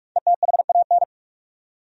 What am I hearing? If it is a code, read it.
Morse code, heard as a keyed tone, about 46 wpm, 700 Hz. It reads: ETHAN
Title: USA Common Names - Max 5 Letters - 45wpm